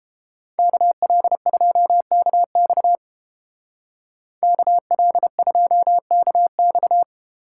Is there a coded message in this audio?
Yes: Morse code from 0.6 to 7.0 s, heard as a keyed tone, about 32 wpm, 700 Hz. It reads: KL2KX KL2KX